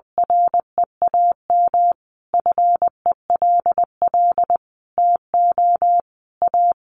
Morse code, heard as a keyed tone, about 20 wpm, 700 Hz. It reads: REAM FELL TO A
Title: A Sound Of Thunder 20wpm